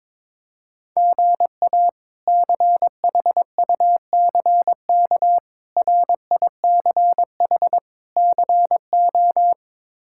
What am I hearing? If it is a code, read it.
Morse code, heard as a keyed tone, about 22 wpm, 700 Hz. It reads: GA CHUCK RICH CO
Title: K1USN SST QSO Practice - 22wpm